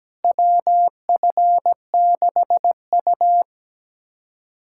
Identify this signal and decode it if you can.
Morse code, heard as a keyed tone, about 17 wpm, 700 Hz. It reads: WF6U
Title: USA Callsign DE Callsign - 4 Characters - 17wpm